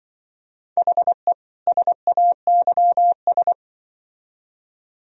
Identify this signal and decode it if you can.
Morse code, heard as a keyed tone, about 24 wpm, 700 Hz. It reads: HE SAYS
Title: Speed-Racing - Sets of 2 Words 20wpm